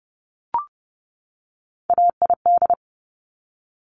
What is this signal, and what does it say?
Morse code, heard as a keyed tone, about 30 wpm, 700 Hz. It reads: AID
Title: Top Three Letter Combinations in Words 30wpm